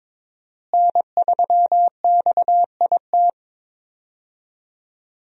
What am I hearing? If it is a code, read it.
Morse code, heard as a keyed tone, about 22 wpm, 700 Hz. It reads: N3XIT
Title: USA Call Signs - 5 Characters - Repeated 3x - 22wpm